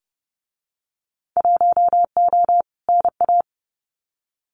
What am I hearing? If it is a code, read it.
Morse code, heard as a keyed tone, about 31 wpm, 700 Hz. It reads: 1O NA
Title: Winter Field Day Exchanges 30wpm